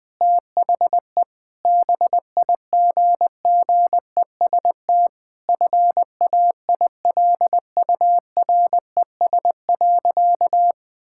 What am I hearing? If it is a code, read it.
Morse code, heard as a keyed tone, about 21 wpm, 700 Hz. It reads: THE BIGGEST FAILURES.
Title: Sentences with 7-Word Phrases - Spoken Beginning - 20wpm